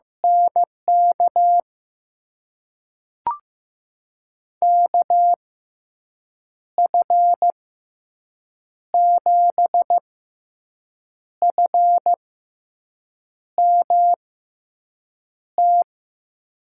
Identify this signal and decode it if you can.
Morse code, heard as a keyed tone, about 15 wpm, spaced as for 6 wpm, 700 Hz. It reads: NK KF7FMT
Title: USA Call Signs - 6 Characters - 6x Character Spacing - 15wpm